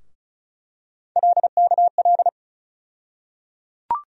Morse code, heard as a keyed tone, about 34 wpm, 700 Hz. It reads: LKL
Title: USA Call Sign Postfixes - 3 Letters 35wpm